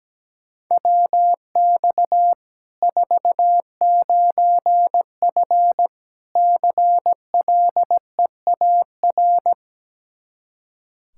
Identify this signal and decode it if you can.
Morse code, heard as a keyed tone, about 17 wpm, 700 Hz. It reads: WX 49F CLEAR
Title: Practice QSO - Exchange 2 - Info - 17wpm